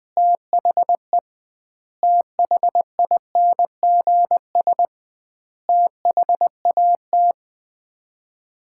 Morse code, heard as a keyed tone, about 20 wpm, 700 Hz. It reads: THE THINGS THAT
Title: 2x Word-Spacing - Sets of 5 Words - 20wpm